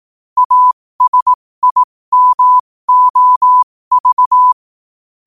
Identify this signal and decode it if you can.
Morse code, heard as a keyed tone, about 18 wpm, spaced as for 15 wpm, 1000 Hz. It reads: ASIMOV
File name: qotdfiles / 15wpm